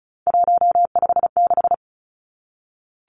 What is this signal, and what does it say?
Morse code, heard as a keyed tone, about 34 wpm, 700 Hz. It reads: 156
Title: Numbers - 3-Digit 35wpm